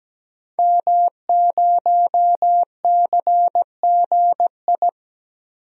Morse code, heard as a keyed tone, about 17 wpm, 700 Hz. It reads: M0CGI
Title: DXCC-100 Call Signs - 5 Characters - 17wpm